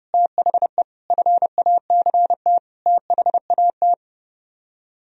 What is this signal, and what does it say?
Morse code, heard as a keyed tone, about 38 wpm, 700 Hz. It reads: THE FACT THAT